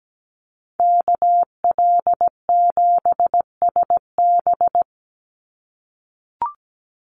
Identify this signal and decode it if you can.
Morse code, heard as a keyed tone, about 17 wpm, 700 Hz. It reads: KL7SB
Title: Call Signs - 17wpm